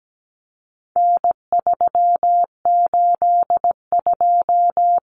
Morse code, heard as a keyed tone, about 17 wpm, 700 Hz. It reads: N382
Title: Common Cut Numbers - 4-Digit 17wpm